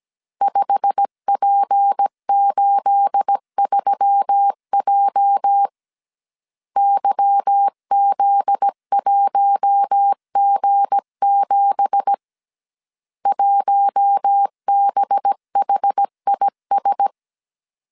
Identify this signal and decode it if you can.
Morse code, heard as a keyed tone, about 17 wpm, 800 Hz. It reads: 5P83J YZ1G7 1BHIS
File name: CW / DET / 1 9 1500C S A 16wpm 2828